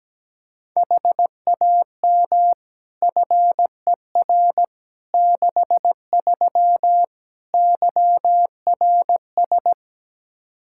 Morse code, heard as a keyed tone, about 17 wpm, 700 Hz. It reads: HAM FER 63 YRS
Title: Practice QSO - Exchange 3 - More Info - 17wpm